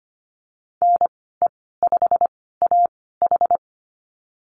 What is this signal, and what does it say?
Morse code, heard as a keyed tone, about 25 wpm, spaced as for 16 wpm, 690 Hz. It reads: NE5AH
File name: Lesson 15g - Number 5 - Call Signs 25wpm